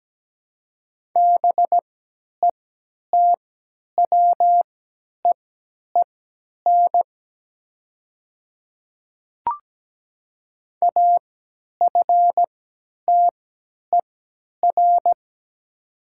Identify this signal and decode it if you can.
Morse code, heard as a keyed tone, about 17 wpm, spaced as for 10 wpm, 700 Hz. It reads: BETWEEN AFTER
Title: 3x Character Spacing - Top 300 Words - 17wpm